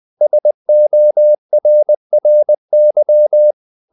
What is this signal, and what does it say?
Morse code, heard as a keyed tone, about 20 wpm, 600 Hz. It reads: SORRY
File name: the old man and the sea part 2 20wpm